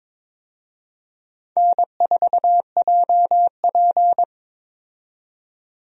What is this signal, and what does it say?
Morse code, heard as a keyed tone, about 22 wpm, 700 Hz. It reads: N4JP
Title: USA Call Signs - 4 Characters - 22wpm